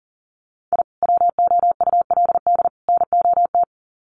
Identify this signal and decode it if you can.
Morse code, heard as a keyed tone, about 40 wpm, 700 Hz. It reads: I WOULD NOT
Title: Sets of 3 Words 40wpm